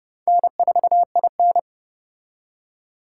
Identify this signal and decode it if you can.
Morse code, heard as a keyed tone, about 29 wpm, 700 Hz. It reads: N4IN